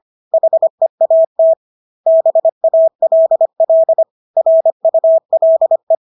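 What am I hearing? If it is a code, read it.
Morse code, heard as a keyed tone, about 25 wpm, 650 Hz. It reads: HEAT BALL RULE
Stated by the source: Morse practice of English words at 25 wpm from CW med Gnister.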